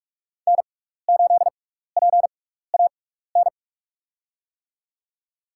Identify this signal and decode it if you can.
Morse code, heard as a keyed tone, about 41 wpm, spaced as for 15 wpm, 690 Hz. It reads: N8PAN
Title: USA Call Signs - 5 Characters - 6x Character Spacing - 45wpm